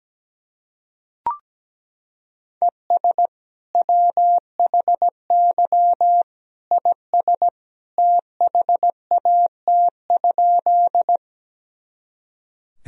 Morse code, heard as a keyed tone, about 17 wpm, 700 Hz. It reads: ES WHY IS THAT?
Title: Sentences from Top 200 Words and Abbreviations 17wpm